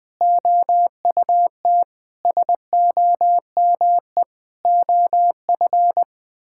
Morse code, heard as a keyed tone, about 20 wpm, 700 Hz. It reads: OUT SOME OF
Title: Sets of 9 Words - 20wpm